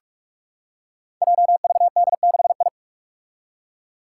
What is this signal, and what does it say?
Morse code, heard as a keyed tone, about 46 wpm, 700 Hz. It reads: JVD6I
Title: Five Challenging Letter-Number Combinations 45wpm